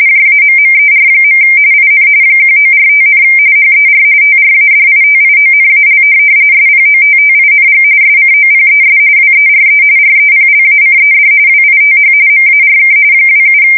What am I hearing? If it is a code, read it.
RTTY, about 45.45 baud. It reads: JAMMER. IN 1945, HER TRADITIONAL SPOTTING SCOPES WERE REPLACED WITH MARK 27